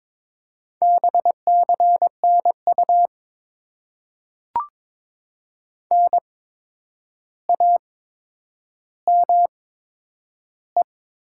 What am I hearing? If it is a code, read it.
Morse code, heard as a keyed tone, about 22 wpm, spaced as for 9 wpm, 700 Hz. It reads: BCNU NAME
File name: Top 90 QSO Elements - 8x Character Spacing - 22wpm